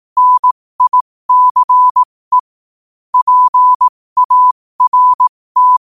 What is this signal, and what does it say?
Morse code, heard as a keyed tone, about 18 wpm, spaced as for 15 wpm, 1000 Hz. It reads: NICE PART
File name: qotdfiles / 15wpm